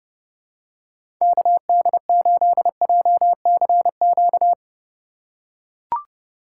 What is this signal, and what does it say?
Morse code, heard as a keyed tone, about 30 wpm, 700 Hz. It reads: KD8JCQ
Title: USA Call Signs - 6 Characters - 3x Character Spacing - 30wpm